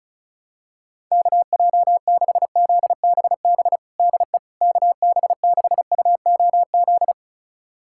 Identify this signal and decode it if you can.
Morse code, heard as a keyed tone, about 36 wpm, 700 Hz. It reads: KJ6ZBB DE KB6UOZ
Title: USA Callsign DE Callsign 35wpm